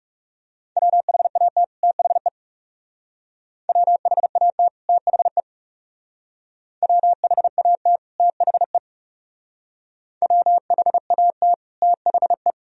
Morse code, heard as a keyed tone, about 36 wpm, 700 Hz. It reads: WHAT THE WHAT THE WHAT THE WHAT THE
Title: Speed-Racing - Sets of 2 Words 30wpm